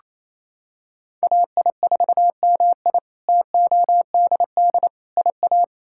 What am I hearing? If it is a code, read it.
Morse code, heard as a keyed tone, about 27 wpm, 700 Hz. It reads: AI4MI TODD IA
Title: K1USN SST QSO Practice - 28wpm